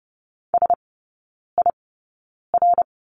Morse code, heard as a keyed tone, about 30 wpm, spaced as for 9 wpm, 690 Hz. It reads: SIR
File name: Lesson 16e - Letter c - Three Characters 30wpm